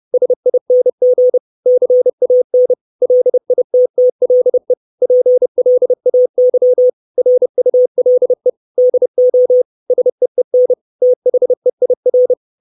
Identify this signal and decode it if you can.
Morse code, heard as a keyed tone, about 30 wpm, 500 Hz. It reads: SING CAN LITTLE PLAY RULE DO SEEN THEIR
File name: 632 Words 30WPM